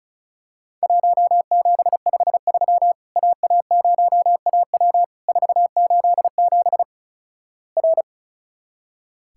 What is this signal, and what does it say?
Morse code, heard as a keyed tone, about 36 wpm, 700 Hz. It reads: 1753 AA0AW 487 R
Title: Worked All Europe DX Contest - QTC Exchange - 35wpm